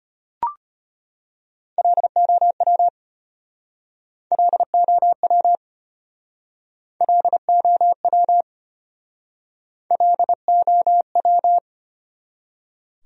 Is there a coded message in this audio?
Yes: Morse code from 1.8 to 11.6 s, heard as a keyed tone, about 32 wpm, 700 Hz. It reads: LOW LOW LOW LOW